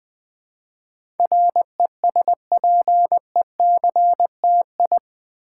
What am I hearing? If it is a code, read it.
Morse code, heard as a keyed tone, about 20 wpm, 700 Hz. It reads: RESPECTI
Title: Build A Word - 4 Letters At A Time - 12-Letter Words - 20wpm